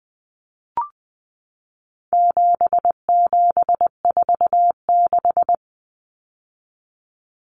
Morse code, heard as a keyed tone, about 20 wpm, 700 Hz. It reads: 7746
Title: Numbers - 4-Digit 20wpm